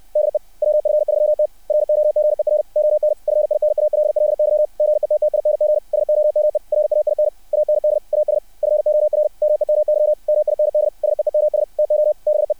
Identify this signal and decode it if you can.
Morse code, heard as a keyed tone, about 60 wpm, 600 Hz. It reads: DE WF3T ROGER ON FINALLY GETTING SOME RAIN AND AS FOR ME MY MIND SEEMS TO BE
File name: archived / CRSnet